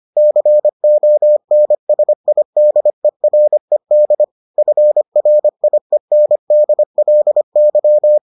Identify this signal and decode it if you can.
Morse code, heard as a keyed tone, about 25 wpm, 600 Hz. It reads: CONSIDERED FRIENDLY